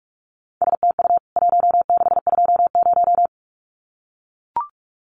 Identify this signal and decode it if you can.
Morse code, heard as a keyed tone, about 44 wpm, 700 Hz. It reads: STU 1620
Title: CWT Exchange 45wpm